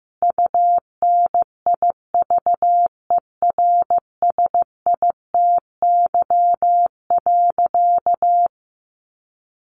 Morse code, heard as a keyed tone, about 15 wpm, 700 Hz. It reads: UNIVERSITY.
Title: Sentences from Top 1300 Words 15wpm